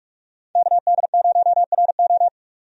Morse code, heard as a keyed tone, about 43 wpm, 700 Hz. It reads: KD0RO
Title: USA Call Signs - 5 Characters - 45wpm